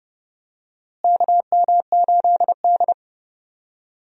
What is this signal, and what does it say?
Morse code, heard as a keyed tone, about 30 wpm, 700 Hz. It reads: KM8D
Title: USA Call Signs - 4 Characters - Repeated 2x - 30wpm